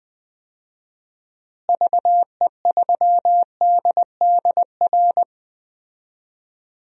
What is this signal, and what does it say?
Morse code, heard as a keyed tone, about 20 wpm, 700 Hz. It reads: VE3DDR
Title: DXCC-100 Call Signs - 6 Characters - Repeated 3x - 20wpm